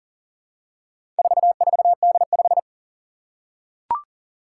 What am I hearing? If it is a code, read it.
Morse code, heard as a keyed tone, about 39 wpm, 700 Hz. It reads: 44D5
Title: Four Challenging Letter-Number Combinations 40wpm